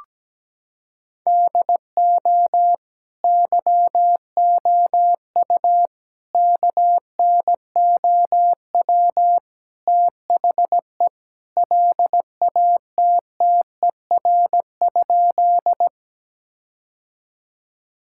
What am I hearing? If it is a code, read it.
Morse code, heard as a keyed tone, about 17 wpm, 700 Hz. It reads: DO YOU KNOW THE LATTER?
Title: Sentences from Top 1600 Words 17wpm